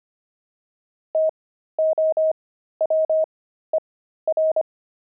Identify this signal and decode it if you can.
Morse code, heard as a keyed tone, about 25 wpm, spaced as for 13 wpm, 630 Hz. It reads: TOWER
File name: pangram-fs-13-cs-25-jet